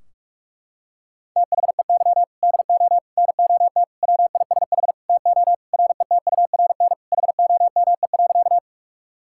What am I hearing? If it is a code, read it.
Morse code, heard as a keyed tone, about 41 wpm, 700 Hz. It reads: THEY DO NOT WISH TO RETURN HOME.
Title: Sentences from Top 1600 Words 45wpm